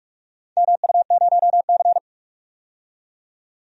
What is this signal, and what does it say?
Morse code, heard as a keyed tone, about 45 wpm, 700 Hz. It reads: MU0C
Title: DXCC-100 Call Signs - 4 Characters - 45wpm